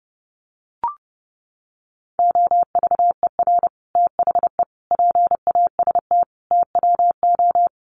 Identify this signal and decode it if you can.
Morse code, heard as a keyed tone, about 30 wpm, 700 Hz. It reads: OVER THE PAST TWO
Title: Sets of 5 Words Encore2 30wpm